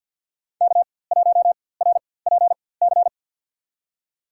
Morse code, heard as a keyed tone, about 51 wpm, spaced as for 23 wpm, 700 Hz. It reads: K1RPC
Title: USA Call Signs - 5 Characters - 4x Character Spacing - 50wpm